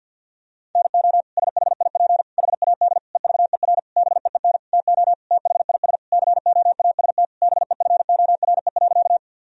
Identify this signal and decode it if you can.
Morse code, heard as a keyed tone, about 47 wpm, 700 Hz. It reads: NO SHIP HAD EVER BEEN TO THIS COAST BEFORE.
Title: Sentences from Top 1000 Words - Review Part 4 - 50wpm